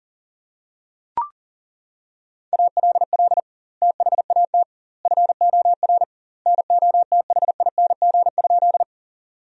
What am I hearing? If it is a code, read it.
Morse code, heard as a keyed tone, about 40 wpm, 700 Hz. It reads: ALL THAT FOR NOTHING?
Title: 2x Word-Spacing - Sentences from Top 500 Words - 40wpm